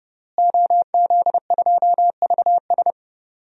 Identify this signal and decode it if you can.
Morse code, heard as a keyed tone, about 31 wpm, 700 Hz. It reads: OZ2VS
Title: DXCC-100 Call Signs - 5 Characters - Repeated 2x - 30wpm